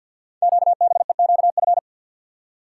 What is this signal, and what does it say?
Morse code, heard as a keyed tone, about 49 wpm, 700 Hz. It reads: QBEQF
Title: Code Groups - 5 Letters - 50wpm